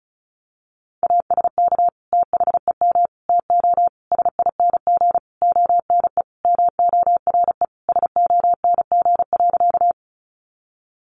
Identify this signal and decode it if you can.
Morse code, heard as a keyed tone, about 34 wpm, 700 Hz. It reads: ASK THEM TO SING ONE MORE SONG.
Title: Sentences from Top 400 Words 35wpm